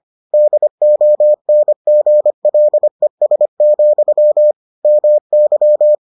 Morse code, heard as a keyed tone, about 25 wpm, 600 Hz. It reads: DONGLES, MY